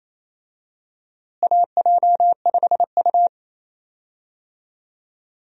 Morse code, heard as a keyed tone, about 27 wpm, 700 Hz. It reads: AJ5U
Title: USA Call Signs - 4 Characters - Repeated 3x - 28wpm